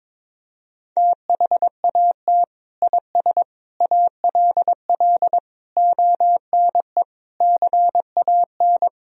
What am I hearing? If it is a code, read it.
Morse code, heard as a keyed tone, about 22 wpm, 700 Hz. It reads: THAT IS ALL ONE CAN
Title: Sentences from Top 200 Words and Abbreviations 22wpm